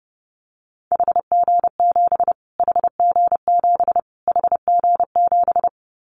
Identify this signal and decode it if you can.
Morse code, heard as a keyed tone, about 30 wpm, 690 Hz. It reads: HG7 HG7 HG7
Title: Callsign Prefixes for Top 25 DXCC - Repeated 3x - 30wpm